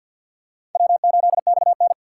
Morse code, heard as a keyed tone, about 46 wpm, 700 Hz. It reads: W8XN
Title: USA Call Signs - 4 Characters - 6x Character Spacing - 50wpm